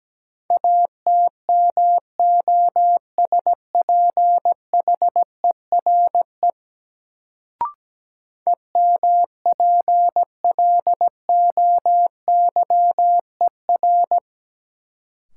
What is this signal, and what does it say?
Morse code, heard as a keyed tone, about 17 wpm, 700 Hz. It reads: ATMOSPHERE EMPLOYER
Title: Sentences from Top 1600 Words - New Word Drill Alt - 17wpm